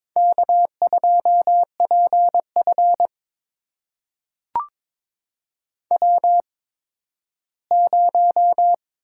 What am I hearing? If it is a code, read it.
Morse code, heard as a keyed tone, about 22 wpm, spaced as for 13 wpm, 700 Hz. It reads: K2PF W0
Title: USA Call Signs - 4 Characters - 8x Character Spacing - 22wpm